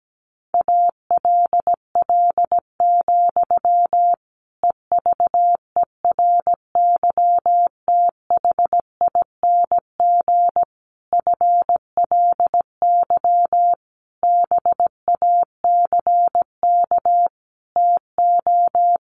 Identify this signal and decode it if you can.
Morse code, heard as a keyed tone, about 17 wpm, 700 Hz. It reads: ALL, EVERYTHING FLY BACK TO
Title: A Sound Of Thunder 17wpm